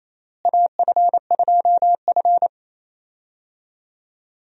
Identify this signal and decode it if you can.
Morse code, heard as a keyed tone, about 28 wpm, 700 Hz. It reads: AF2F
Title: USA Call Signs - 4 Characters - Repeated 3x - 28wpm